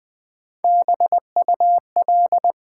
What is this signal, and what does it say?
Morse code, heard as a keyed tone, about 20 wpm, 700 Hz. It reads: BUL
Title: Build A Word - 1 Letter At A Time - 4-Letter Words - 20wpm